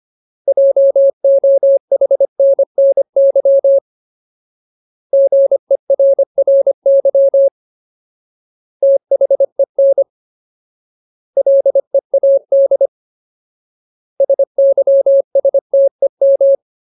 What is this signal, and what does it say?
Morse code, heard as a keyed tone, about 25 wpm, 550 Hz. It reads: JOHNNY GERRY THEN LEAD SYSTEM
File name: Ultimate Word List / 25WPM Ultimate